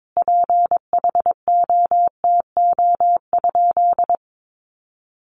Morse code, heard as a keyed tone, about 22 wpm, 700 Hz. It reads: PHOTO?
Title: Sentences from Top 1200 Words 22wpm